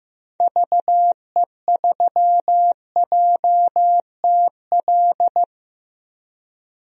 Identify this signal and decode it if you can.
Morse code, heard as a keyed tone, about 15 wpm, 690 Hz. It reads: VE3JTL
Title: DXCC-100 Call Signs - 6 Characters - 15wpm